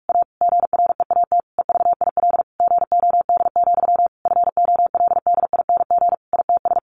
Morse code, heard as a keyed tone, about 44 wpm, 700 Hz. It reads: A GREAT EVIL GOD, FOLDING ITS
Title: A Sound Of Thunder 45wpm